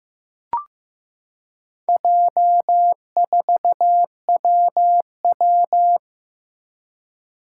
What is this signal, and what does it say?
Morse code, heard as a keyed tone, about 15 wpm, 700 Hz. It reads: J4WW
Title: Four Challenging Letter-Number Combinations 15wpm